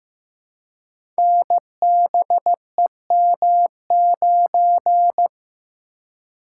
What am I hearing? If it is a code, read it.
Morse code, heard as a keyed tone, about 15 wpm, 700 Hz. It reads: NBEM9
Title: Code Groups - 5 Letter:Numbers - 15wpm